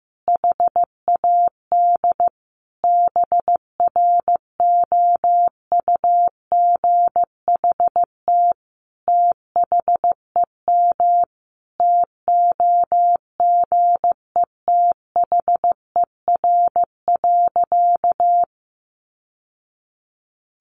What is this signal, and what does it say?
Morse code, heard as a keyed tone, about 15 wpm, 700 Hz. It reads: HAD BROUGHT THEM TOGETHER.
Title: Sentences from Top 1000 Words 15wpm